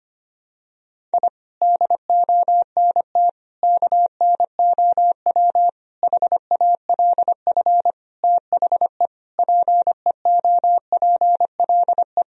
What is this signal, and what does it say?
Morse code, heard as a keyed tone, about 25 wpm, 700 Hz. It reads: I DONT KNOW HALF THE PEOPLE
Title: Sentences from Top 1000 Words - Review Part 1 - 25wpm